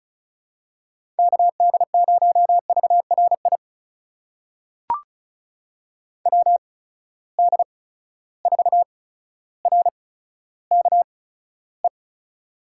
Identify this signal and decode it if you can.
Morse code, heard as a keyed tone, about 36 wpm, spaced as for 15 wpm, 700 Hz. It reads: KD0VRI WD4RKE
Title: USA Call Signs - 6 Characters - 8x Character Spacing - 35wpm